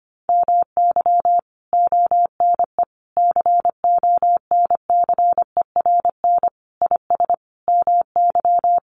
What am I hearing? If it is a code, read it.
Morse code, heard as a keyed tone, about 25 wpm, 700 Hz. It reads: MY ONE CONCERN IS MY
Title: Sentences from Top 1200 Words 25wpm